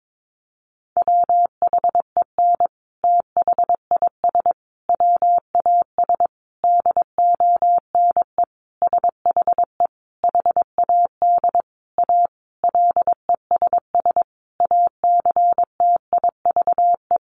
Morse code, heard as a keyed tone, about 22 wpm, 700 Hz. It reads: WHEN THIS WAS DONE SHE HAD A LESS ACTIVE
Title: Sentences from Top 900 Words 22wpm